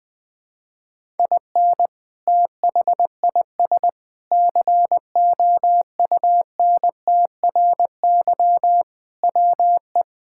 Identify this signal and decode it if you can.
Morse code, heard as a keyed tone, about 21 wpm, 700 Hz. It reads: IN THIS COUNTRY WE
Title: Sentences with 3-Word Phrases - 20wpm